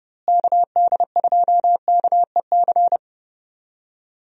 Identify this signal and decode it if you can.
Morse code, heard as a keyed tone, about 31 wpm, 700 Hz. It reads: KD2KEC
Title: USA Call Signs - 6 Characters - Repeated 2x - 30wpm